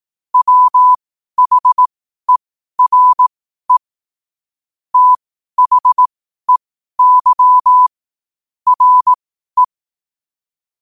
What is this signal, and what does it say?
Morse code, heard as a keyed tone, about 18 wpm, spaced as for 12 wpm, 1000 Hz. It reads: WHERE THEY RE
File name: qotdfiles / 12wpm